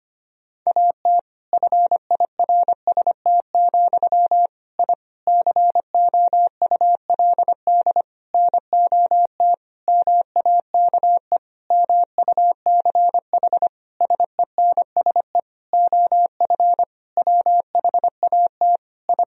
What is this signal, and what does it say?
Morse code, heard as a keyed tone, about 25 wpm, 700 Hz. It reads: AT FIRST, I COULD NOT MAKE MUCH SENSE OF WHAT I
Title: Sentences from Top 1000 Words - Review Part 4 - 25wpm